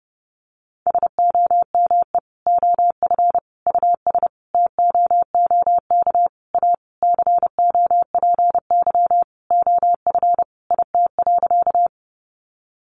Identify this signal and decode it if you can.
Morse code, heard as a keyed tone, about 30 wpm, 700 Hz. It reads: SOME OF US TOOK A COPY OF IT.
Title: Sentences from Top 1400 Words 30wpm